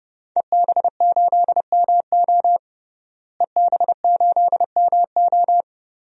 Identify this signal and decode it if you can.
Morse code, heard as a keyed tone, about 32 wpm, 700 Hz. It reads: EB8MO EB8MO
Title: DXCC-100 Call Signs - 5 Characters - Repeated 2x - 30wpm